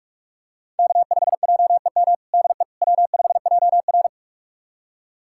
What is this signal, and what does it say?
Morse code, heard as a keyed tone, about 46 wpm, 700 Hz. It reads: K5JEM DE W5JR
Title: USA Callsign DE Callsign 45wpm